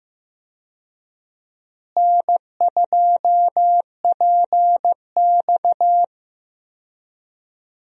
Morse code, heard as a keyed tone, about 15 wpm, 700 Hz. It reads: N2PX
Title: USA Call Signs - 4 Characters - Repeated 3x - 15wpm